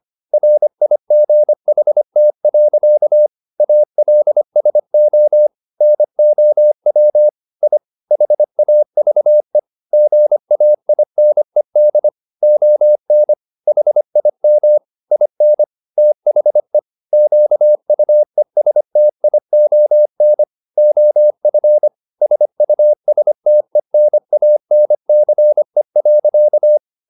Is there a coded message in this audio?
Yes: Morse code from 0.3 to 26.8 s, heard as a keyed tone, about 25 wpm, 600 Hz. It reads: RIGHT. ALSO NOW I HAVE GAINED ON HIM IN THE QUESTION OF SUSTENANCE.